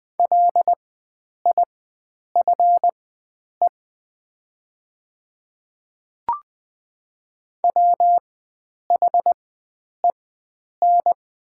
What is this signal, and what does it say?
Morse code, heard as a keyed tone, about 20 wpm, spaced as for 9 wpm, 700 Hz. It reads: LIFE WHEN
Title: 4x Character Spacing - Top 500 Words - 20wpm